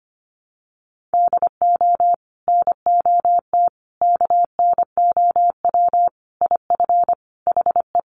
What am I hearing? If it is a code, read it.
Morse code, heard as a keyed tone, about 25 wpm, 700 Hz. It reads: DO NOT KNOW IF HE
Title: Sets of 5 Words Encore2 25wpm